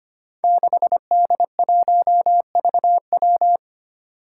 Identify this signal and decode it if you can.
Morse code, heard as a keyed tone, about 25 wpm, 700 Hz. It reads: 6D1VW
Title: Five Challenging Letter-Number Combinations 25wpm